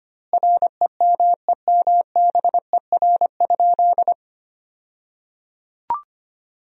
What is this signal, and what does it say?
Morse code, heard as a keyed tone, about 25 wpm, 700 Hz. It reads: REMEMBER?
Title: 2x Word-Spacing - Sentences from Top 400 Words - 25wpm